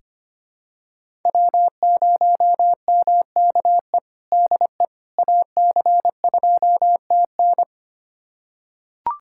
Morse code, heard as a keyed tone, about 25 wpm, 700 Hz. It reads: W0MKE DE AC2TN
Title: USA Callsign DE Callsign - 5 Characters - 25wpm